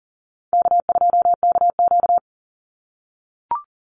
Morse code, heard as a keyed tone, about 38 wpm, 700 Hz. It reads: K2KQ
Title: Call Signs - 40wpm